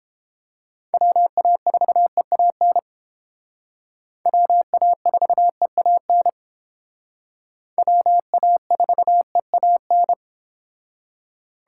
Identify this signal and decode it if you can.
Morse code, heard as a keyed tone, about 29 wpm, 700 Hz. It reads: WA4EAN WA4EAN WA4EAN